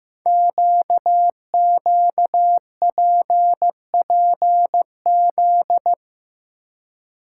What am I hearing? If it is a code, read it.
Morse code, heard as a keyed tone, about 15 wpm, 700 Hz. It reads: QQPPZ
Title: Code Groups - 5 Letters - 15wpm